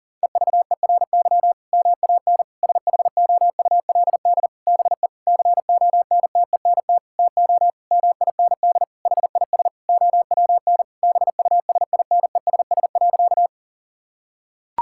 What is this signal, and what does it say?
Morse code, heard as a keyed tone, about 40 wpm, 700 Hz. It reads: EVERY MAN SHOULD BE CONTENT TO MIND HIS OWN BUSINESS.
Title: Sentences from Top 1000 Words - Review Part 6 - 40wpm